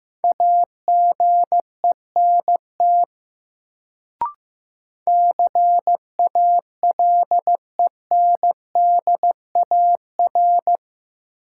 Morse code, heard as a keyed tone, about 15 wpm, 700 Hz. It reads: AGENT CALENDAR
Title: Sentences from Top 1600 Words - New Word Drill - 15wpm